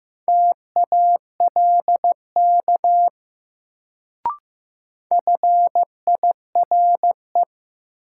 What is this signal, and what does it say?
Morse code, heard as a keyed tone, about 15 wpm, 700 Hz. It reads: TALK FIRE
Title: Sentences from Top 400 Words - New Word Drill - 15wpm